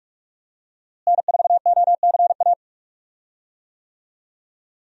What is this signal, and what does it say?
Morse code, heard as a keyed tone, about 47 wpm, 700 Hz. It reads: N4OCA
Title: USA Call Signs - 5 Characters - Repeated 3x - 45wpm